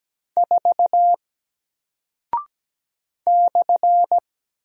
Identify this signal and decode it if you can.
Morse code, heard as a keyed tone, about 17 wpm, 700 Hz. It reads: 4 /
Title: Single Character (Letter:Number:Punctuation:Prosign) Alt 17wpm